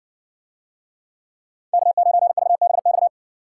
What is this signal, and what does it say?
Morse code, heard as a keyed tone, about 56 wpm, 700 Hz. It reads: K9VBX